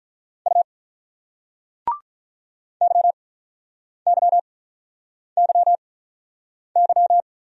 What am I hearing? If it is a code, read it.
Morse code, heard as a keyed tone, about 46 wpm, spaced as for 9 wpm, 700 Hz. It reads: U YYYY